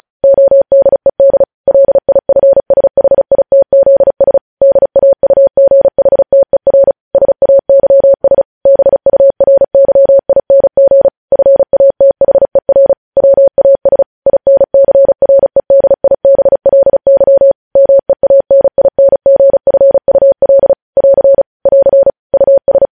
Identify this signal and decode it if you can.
Morse code, heard as a keyed tone, about 36 wpm, 550 Hz. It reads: ODED LIFSHITZS DAUGHTER SAYS BURYING FATHER WAS INCREDIBLY MEANINGFUL <AR> <AR> US
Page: News Headlines in Morse Code at 35 WPM